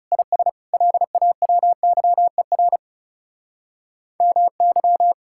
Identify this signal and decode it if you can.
Morse code, heard as a keyed tone, about 34 wpm, 700 Hz. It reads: IS LAWYER MY
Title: Speed-Racing Common QSO Phrases Encore - 30wpm